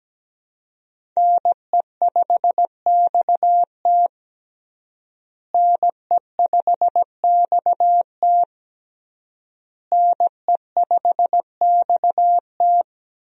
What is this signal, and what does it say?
Morse code, heard as a keyed tone, about 17 wpm, 700 Hz. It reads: NE5XT NE5XT NE5XT
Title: USA Call Signs - 5 Characters - Repeated 3x - 17wpm